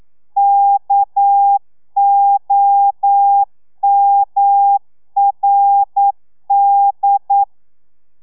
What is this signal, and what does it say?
Morse code, heard as a keyed tone, about 9 wpm, 800 Hz. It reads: KOMRD
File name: telegrafia / i2tfj / Lez6 40